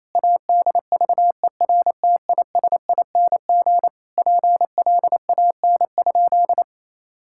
Morse code, heard as a keyed tone, about 28 wpm, 700 Hz. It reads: ADVERTISING PLAN?
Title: Sentences from Top 1000 Words - Review Part 6 - 28wpm